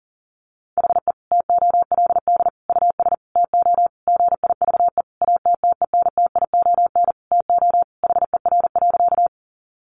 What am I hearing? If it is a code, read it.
Morse code, heard as a keyed tone, about 40 wpm, 700 Hz. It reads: HE TOLD US TO GIVE ATTENTION TO HER.
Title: Sentences from Top 1000 Words 40wpm